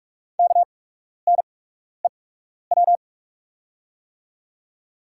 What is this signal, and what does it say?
Morse code, heard as a keyed tone, about 43 wpm, spaced as for 12 wpm, 690 Hz. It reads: KNEW
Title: 8x Character Spacing - Top 500 Words - 45wpm